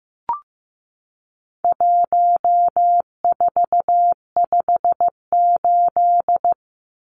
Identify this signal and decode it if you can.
Morse code, heard as a keyed tone, about 15 wpm, 700 Hz. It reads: 1458
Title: Numbers - 4-Digit 15wpm